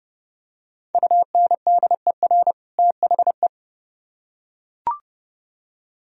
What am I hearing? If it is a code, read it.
Morse code, heard as a keyed tone, about 28 wpm, 700 Hz. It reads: UNDER THE